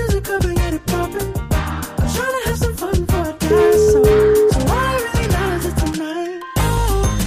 Ringback tone (Europe, 425 Hz), with noise.